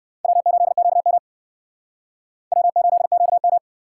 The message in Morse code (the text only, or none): W8QM W8QM